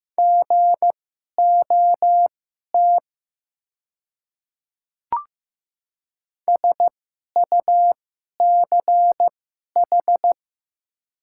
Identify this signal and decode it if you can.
Morse code, heard as a keyed tone, about 15 wpm, spaced as for 11 wpm, 700 Hz. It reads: GOT SUCH
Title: 2x Character Spacing - Top 500 Words - 15wpm